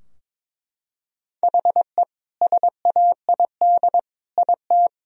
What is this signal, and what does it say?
Morse code, heard as a keyed tone, about 22 wpm, 700 Hz. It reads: HE SAID IT
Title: Sentences from Top 1000 Words - Review Part 2 - 22wpm